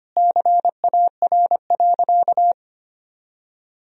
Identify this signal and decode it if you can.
Morse code, heard as a keyed tone, about 25 wpm, 700 Hz. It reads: CAR.
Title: Sentences from Top 1000 Words - Review Part 1 - 25wpm